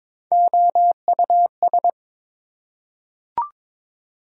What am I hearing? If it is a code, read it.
Morse code, heard as a keyed tone, about 22 wpm, 700 Hz. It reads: OUS
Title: Prefixes and Suffixes 22wpm